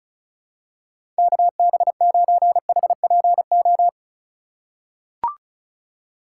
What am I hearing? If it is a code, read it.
Morse code, heard as a keyed tone, about 34 wpm, 690 Hz. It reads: KB9HPO
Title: USA Call Signs - 6 Characters - 4x Character Spacing - 35wpm